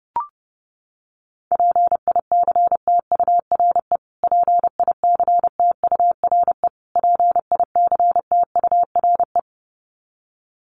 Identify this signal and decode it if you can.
Morse code, heard as a keyed tone, about 30 wpm, 700 Hz. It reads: PICTURE PICTURE PICTURE
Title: Top 300 Words Repeated 30wpm